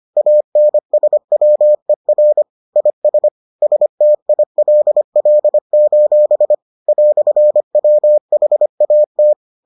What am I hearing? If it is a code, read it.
Morse code, heard as a keyed tone, about 25 wpm, 600 Hz. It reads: ANSWER IS STILL: "WHAT